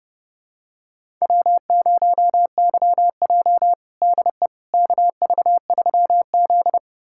Morse code, heard as a keyed tone, about 32 wpm, 700 Hz. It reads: W0YJ DE KV3Z
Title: USA Callsign DE Callsign - 4 Characters - 30wpm